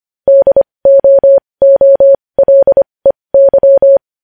Morse code, heard as a keyed tone, about 25 wpm, spaced as for 20 wpm, 550 Hz. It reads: DOOLEY
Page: news-headlines-in-morse-code-at-20-wpm